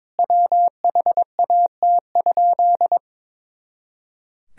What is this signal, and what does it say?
Morse code, heard as a keyed tone, about 22 wpm, 700 Hz. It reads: WHAT?
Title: Sentences from Top 100 Words and Abbreviations 22wpm